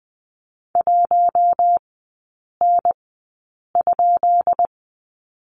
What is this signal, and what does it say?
Morse code, heard as a keyed tone, about 20 wpm, spaced as for 8 wpm, 700 Hz. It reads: 1N?
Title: 178 Lesson 29e - Slash - Three Characters 20wpm